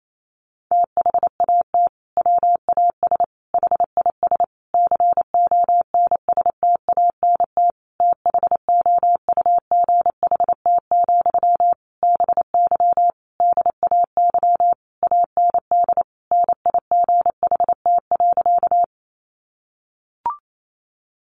Morse code, heard as a keyed tone, about 28 wpm, 700 Hz. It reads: THAT WAS HIS CONSTANT THOUGHT, BY DAY AND NIGHT.
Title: Sentences from Top 1500 Words 28wpm